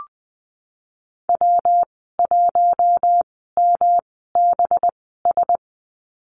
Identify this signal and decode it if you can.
Morse code, heard as a keyed tone, about 20 wpm, spaced as for 15 wpm, 700 Hz. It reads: W1MBS
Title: 238 Lesson 38g - Letter q - Call Signs 20wpm